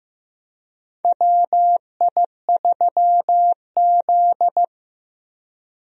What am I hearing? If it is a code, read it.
Morse code, heard as a keyed tone, about 15 wpm, 700 Hz. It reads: WI3Z
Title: USA Call Signs - 4 Characters - Repeated 3x - 15wpm